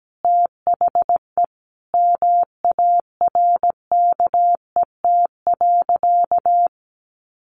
Morse code, heard as a keyed tone, about 17 wpm, 700 Hz. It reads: THE MARKET.
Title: Additional Easy Phrases 17wpm